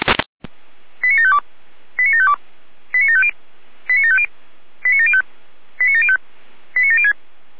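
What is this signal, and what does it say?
Five-tone selective call (ZVEI), heard as quick runs of five tones, tones of about 70 ms. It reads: 89752 89752 89750 89750 89705 89705 89706
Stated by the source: Five-tone ZVEI selective call received on an SDR.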